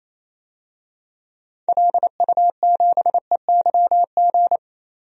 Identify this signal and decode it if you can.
Morse code, heard as a keyed tone, about 28 wpm, 700 Hz. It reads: LU7EYG
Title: DXCC-100 Call Signs - 6 Characters - 28wpm